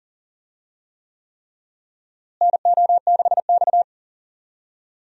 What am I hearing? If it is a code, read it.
Morse code, heard as a keyed tone, about 40 wpm, 700 Hz. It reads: NO6X
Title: USA Call Signs - 4 Characters - Repeated 3x - 40wpm